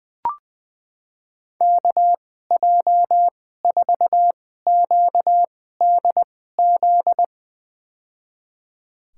Morse code, heard as a keyed tone, about 20 wpm, spaced as for 14 wpm, 700 Hz. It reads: KJ4QDZ
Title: USA Call Signs - 6 Characters - 2x Character Spacing - 20wpm